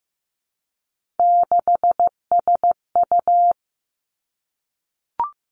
Morse code, heard as a keyed tone, about 15 wpm, 700 Hz. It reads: 6SU
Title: Three Challenging Letter-Number Combinations 15wpm